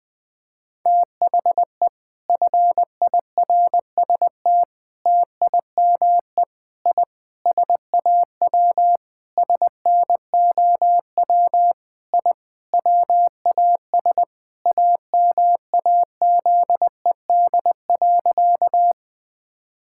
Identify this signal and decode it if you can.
Morse code, heard as a keyed tone, about 20 wpm, 700 Hz. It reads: THE FIRST TIME I SAW SNOW I WAS AMAZED.
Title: Sentences with 4-Word Phrases - 20wpm